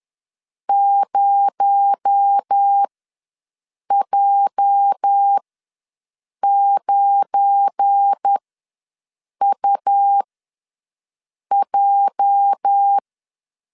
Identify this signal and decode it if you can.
Morse code, heard as a keyed tone, about 10 wpm, spaced as for 6 wpm, 800 Hz. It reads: 0J9UJ
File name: LEZIONI / 8 YJ45 / 8 YJ ws 10wpm 4043